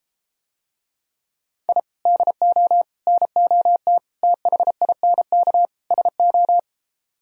Morse code, heard as a keyed tone, about 34 wpm, 700 Hz. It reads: I DO NOT THINK SO